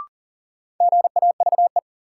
Morse code, heard as a keyed tone, about 40 wpm, 700 Hz. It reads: GAVE
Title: Sentences from Top 500 Words - New Word Drill - 40wpm